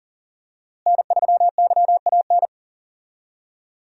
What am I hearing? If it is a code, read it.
Morse code, heard as a keyed tone, about 40 wpm, 700 Hz. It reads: N3YAN
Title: USA Call Signs - 5 Characters - Repeated 3x - 40wpm